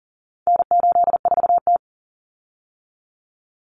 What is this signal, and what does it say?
Morse code, heard as a keyed tone, about 40 wpm, 700 Hz. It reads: N84T
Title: Common Cut Numbers - 4-Digit 40wpm